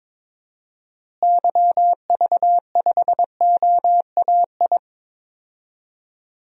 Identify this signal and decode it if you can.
Morse code, heard as a keyed tone, about 22 wpm, 700 Hz. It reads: YV5OAI
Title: DXCC-100 Call Signs - 6 Characters - Repeated 3x - 22wpm